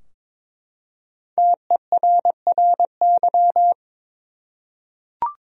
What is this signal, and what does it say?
Morse code, heard as a keyed tone, about 22 wpm, 700 Hz. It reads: TERRY
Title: USA Common Names - Max 5 Letters - 22wpm